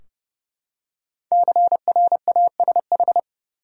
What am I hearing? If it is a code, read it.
Morse code, heard as a keyed tone, about 30 wpm, 700 Hz. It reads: CRASH
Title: Five Letter Words - 30wpm